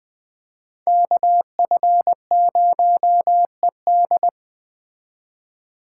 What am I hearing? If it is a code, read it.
Morse code, heard as a keyed tone, about 21 wpm, 700 Hz. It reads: KF0ED